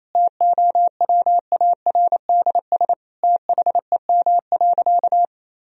Morse code, heard as a keyed tone, about 27 wpm, 700 Hz. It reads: TOWARDS THEM.
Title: Sentences from Top 1000 Words - Review Part 2 - 28wpm